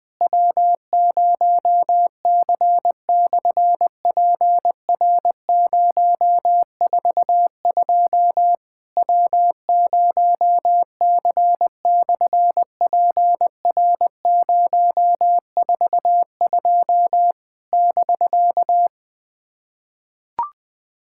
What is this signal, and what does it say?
Morse code, heard as a keyed tone, about 20 wpm, 700 Hz. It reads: W0C/PR042 W0C/PR042 <BK>
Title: SOTA QSO Practice 20wpm